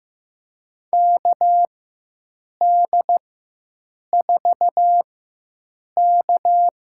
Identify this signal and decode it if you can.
Morse code, heard as a keyed tone, about 15 wpm, spaced as for 7 wpm, 700 Hz. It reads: KD4K
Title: USA Call Signs - 4x Character Spacing - 15wpm